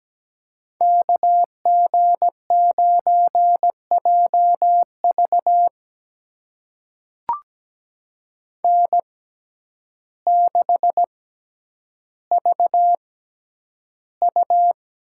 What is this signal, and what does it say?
Morse code, heard as a keyed tone, about 17 wpm, spaced as for 9 wpm, 700 Hz. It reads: KG9JV N6VU
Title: USA Call Signs - 5 Characters - 6x Character Spacing - 17wpm